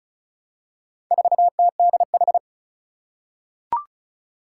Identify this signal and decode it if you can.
Morse code, heard as a keyed tone, about 36 wpm, 700 Hz. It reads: 4TDH
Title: Four Challenging Letter-Number Combinations 35wpm